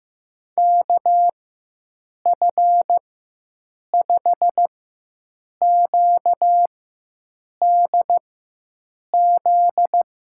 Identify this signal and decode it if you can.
Morse code, heard as a keyed tone, about 15 wpm, spaced as for 7 wpm, 700 Hz. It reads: KF5QDZ
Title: USA Call Signs - 6 Characters - 4x Character Spacing - 15wpm